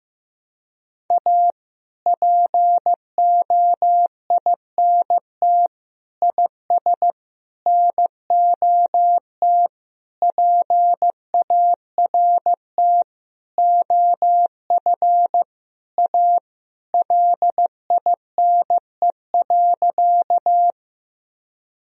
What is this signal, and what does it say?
Morse code, heard as a keyed tone, about 15 wpm, 700 Hz. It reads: A POINT IS NOT PART OF A LINE.
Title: Sentences from Top 1000 Words - Review Part 1 - 15wpm